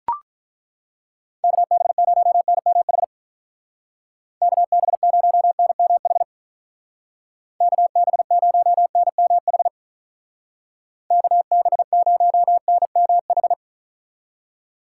Morse code, heard as a keyed tone, about 43 wpm, 700 Hz. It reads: KB0NMH KB0NMH KB0NMH KB0NMH
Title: Speed Racing - USA 2x3 Call Signs 35wpm